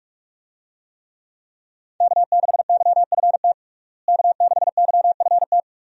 Morse code, heard as a keyed tone, about 43 wpm, 700 Hz. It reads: K6YFT K6YFT
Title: USA Call Signs - 5 Characters - Repeated 3x - 45wpm